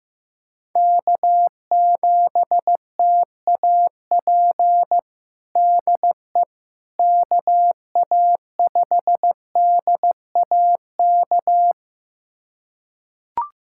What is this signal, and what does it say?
Morse code, heard as a keyed tone, about 16 wpm, 700 Hz. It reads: K7TAP DE KA5DAK
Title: USA Callsign DE Callsign 15wpm